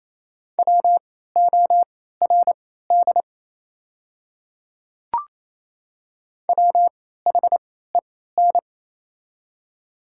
Morse code, heard as a keyed tone, about 28 wpm, spaced as for 16 wpm, 690 Hz. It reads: WORD WHEN
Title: 3x Character Spacing - Top 100 Words - 28wpm